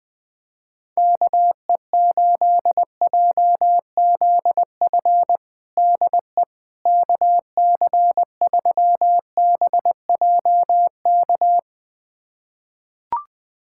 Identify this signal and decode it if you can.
Morse code, heard as a keyed tone, about 21 wpm, 700 Hz. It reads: KE8JZF DE KC3BJK
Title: USA Callsign DE Callsign - 6 Characters - 20wpm